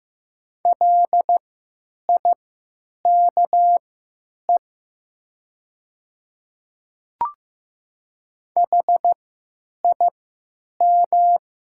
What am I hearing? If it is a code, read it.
Morse code, heard as a keyed tone, about 15 wpm, spaced as for 9 wpm, 700 Hz. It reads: LIKE HIM
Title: 3x Character Spacing - Top 100 Words - 15wpm